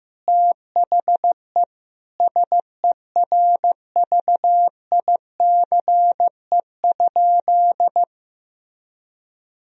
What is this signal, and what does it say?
Morse code, heard as a keyed tone, about 15 wpm, 700 Hz. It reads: THE SERVICE?
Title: Sentences from Top 1000 Words - Review Part 4 - 15wpm